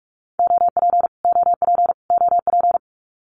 Morse code, heard as a keyed tone, about 44 wpm, 700 Hz. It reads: OP OP OP
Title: Abbreviations Repeated 3x - 45wpm